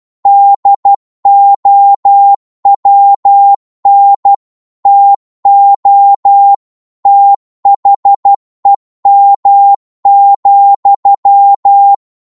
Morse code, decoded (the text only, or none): DOWN TO THEM,